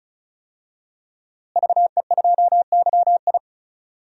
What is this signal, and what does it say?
Morse code, heard as a keyed tone, about 36 wpm, 700 Hz. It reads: VE2YI
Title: DXCC-100 Call Signs - 5 Characters - Repeated 3x - 35wpm